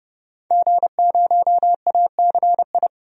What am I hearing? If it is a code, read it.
Morse code, heard as a keyed tone, about 30 wpm, 700 Hz. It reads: G0ACI
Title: DXCC-100 Call Signs - 5 Characters - 30wpm